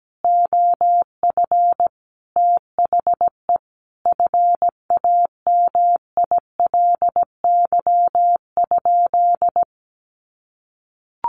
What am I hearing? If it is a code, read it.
Morse code, heard as a keyed tone, about 17 wpm, 700 Hz. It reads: OF THE FAMILY?
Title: Sentences from Top 1100 Words 17wpm